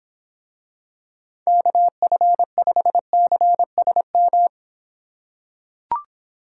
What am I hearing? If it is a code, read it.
Morse code, heard as a keyed tone, about 26 wpm, 700 Hz. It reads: KF5CSM